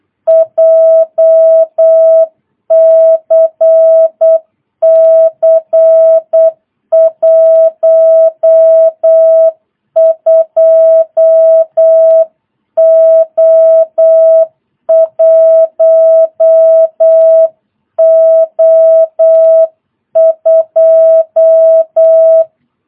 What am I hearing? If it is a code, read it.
Morse code, heard as a keyed tone, about 8 wpm, 650 Hz. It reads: JCC 12O1O2
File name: jh8cbh / 2022cw